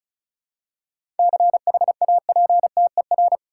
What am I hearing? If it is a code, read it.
Morse code, heard as a keyed tone, about 34 wpm, 700 Hz. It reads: CHAPTER
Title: Seven Letter Words - 35wpm